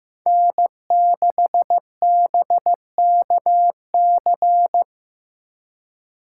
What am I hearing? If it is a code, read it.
Morse code, heard as a keyed tone, about 18 wpm, 700 Hz. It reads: N6BKC
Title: Speed Racing - USA Call Signs 15wpm